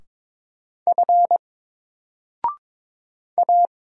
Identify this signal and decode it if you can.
Morse code, heard as a keyed tone, about 22 wpm, 700 Hz. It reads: F A
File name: Single-Letter-Number-22wpm